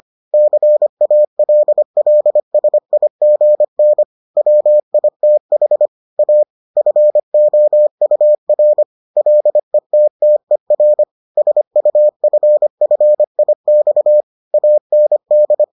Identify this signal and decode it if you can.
Morse code, heard as a keyed tone, about 25 wpm, 600 Hz. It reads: CALLSIGN WITH A FOUR LETTER SUFFIX AND